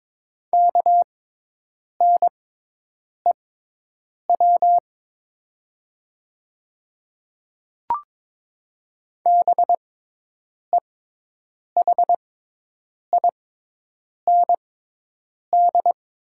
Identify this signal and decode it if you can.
Morse code, heard as a keyed tone, about 22 wpm, spaced as for 8 wpm, 700 Hz. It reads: KNEW BEHIND
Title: 6x Character Spacing - Top 500 Words - 22wpm